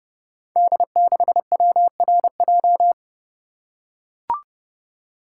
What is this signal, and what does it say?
Morse code, heard as a keyed tone, about 29 wpm, 700 Hz. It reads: D6WRJ
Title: Code Groups - 5 Letter:Numbers - 30wpm